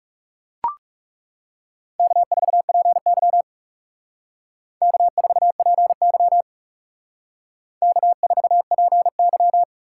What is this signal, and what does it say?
Morse code, heard as a keyed tone, about 39 wpm, 700 Hz. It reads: K4PY K4PY K4PY